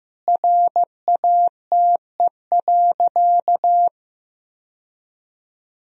Morse code, heard as a keyed tone, about 15 wpm, 700 Hz. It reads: RATE.
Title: Sentences from Top 1000 Words - Review Part 4 - 15wpm